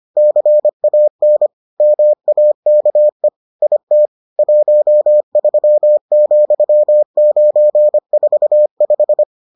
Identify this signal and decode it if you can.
Morse code, heard as a keyed tone, about 25 wpm, 600 Hz. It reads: CAN MAKE IT 13,945